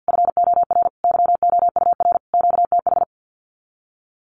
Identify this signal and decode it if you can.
Morse code, heard as a keyed tone, about 50 wpm, 700 Hz. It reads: FOR YOUR QTH
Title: Common QSO Phrases 50wpm